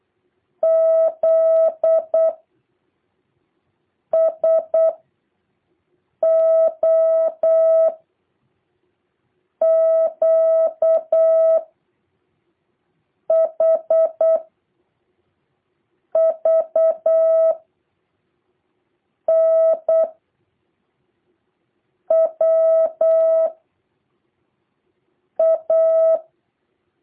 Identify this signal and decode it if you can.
Morse code, heard as a keyed tone, about 8 wpm, 650 Hz. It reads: Z S O Q H V N W A
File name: jh8cbh / 2023cw